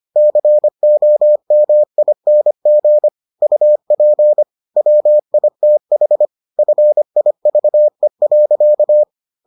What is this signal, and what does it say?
Morse code, heard as a keyed tone, about 25 wpm, 600 Hz. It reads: COMING UP WITH FIVE.